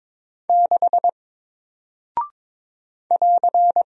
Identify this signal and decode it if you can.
Morse code, heard as a keyed tone, about 22 wpm, 700 Hz. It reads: 6 <AR>
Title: Single Character (Letter:Number:Punctuation:Prosign) Alt 22wpm